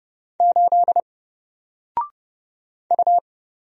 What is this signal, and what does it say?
Morse code, heard as a keyed tone, about 30 wpm, 700 Hz. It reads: 8 U
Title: Single Character (Letter:Number:Punctuation:Prosign) 30wpm